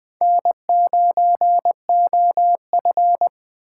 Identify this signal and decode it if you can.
Morse code, heard as a keyed tone, about 23 wpm, 700 Hz. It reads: N9OF
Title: Speed Racing - USA Call Signs 20wpm